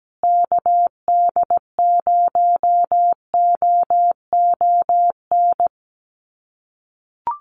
Morse code, heard as a keyed tone, about 17 wpm, 700 Hz. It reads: KD0OON
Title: USA Call Signs Repeated 2x 17wpm